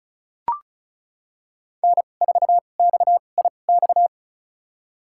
Morse code, heard as a keyed tone, about 36 wpm, spaced as for 25 wpm, 700 Hz. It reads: N4XIX
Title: USA Call Signs - 2x Character Spacing - 35wpm